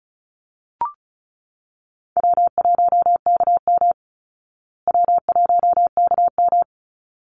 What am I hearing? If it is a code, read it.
Morse code, heard as a keyed tone, about 34 wpm, 700 Hz. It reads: W1KM W1KM
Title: Call Signs Repeated 3x - 35wpm